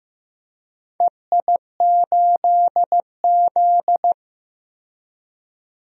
Morse code, heard as a keyed tone, about 15 wpm, 700 Hz. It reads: EI8Z
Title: DXCC-100 Call Signs - 4 Characters - 15wpm